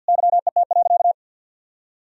Morse code, heard as a keyed tone, about 52 wpm, 700 Hz. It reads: YET.
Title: Sentences from Top 1000 Words - Review Part 2 - 50wpm